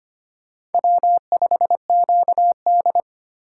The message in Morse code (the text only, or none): W5QD